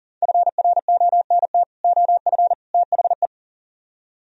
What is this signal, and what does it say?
Morse code, heard as a keyed tone, about 53 wpm, 700 Hz. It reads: FRONT OF THE